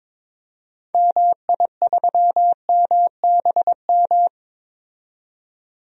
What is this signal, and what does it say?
Morse code, heard as a keyed tone, about 22 wpm, 700 Hz. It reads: MI3MBM
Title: DXCC-100 Call Signs - 6 Characters - 22wpm